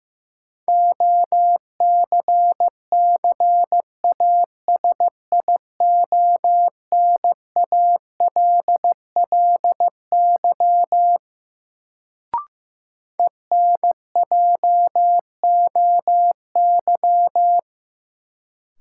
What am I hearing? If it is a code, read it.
Morse code, heard as a keyed tone, about 16 wpm, 700 Hz. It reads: OCCASIONALLY ENJOY
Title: Sentences from Top 1300 Words - New Word Drill Alt - 15wpm